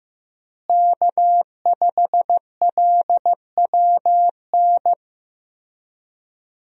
Morse code, heard as a keyed tone, about 15 wpm, 700 Hz. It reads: K5LWN
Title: USA Call Signs - 5 Characters - Repeated 3x - 15wpm